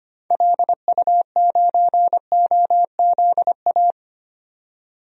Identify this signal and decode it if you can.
Morse code, heard as a keyed tone, about 25 wpm, 700 Hz. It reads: LU9OZA
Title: DXCC-100 Call Signs - 6 Characters - 25wpm